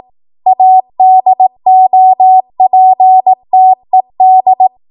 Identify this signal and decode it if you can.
Morse code, heard as a keyed tone, about 18 wpm, 750 Hz. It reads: ADOPTED
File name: Morse / 18WPM